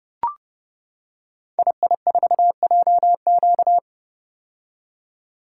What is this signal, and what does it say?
Morse code, heard as a keyed tone, about 30 wpm, 700 Hz. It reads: II4JQ
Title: Code Groups - 5 Letter:Numbers - 30wpm